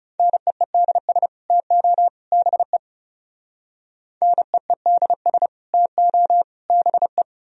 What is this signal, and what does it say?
Morse code, heard as a keyed tone, about 37 wpm, 700 Hz. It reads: NEEDS TO BE NEEDS TO BE
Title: Speed-Racing - Sets of 3 Words 30wpm